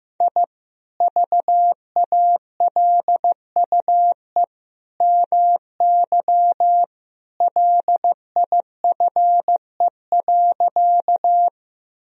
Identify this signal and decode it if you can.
Morse code, heard as a keyed tone, about 15 wpm, 700 Hz. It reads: I VALUE MY LIFE.
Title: Sentences from Top 1000 Words - Review Part 3 - 15wpm